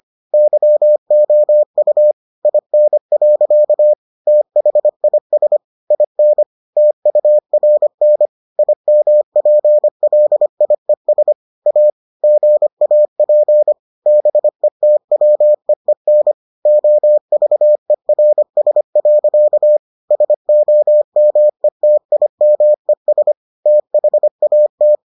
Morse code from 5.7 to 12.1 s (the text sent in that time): IN TURN IMPLIES A